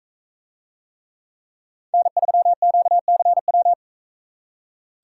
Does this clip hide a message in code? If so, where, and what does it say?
Yes: Morse code from 1.9 to 3.7 s, heard as a keyed tone, about 41 wpm, 700 Hz. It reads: N3QCW